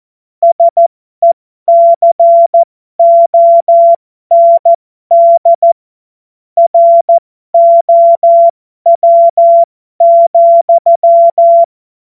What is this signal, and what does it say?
Morse code, heard as a keyed tone, about 14 wpm, spaced as for 12 wpm, 680 Hz. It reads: SECOND ROW,
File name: Morsekurs / Witze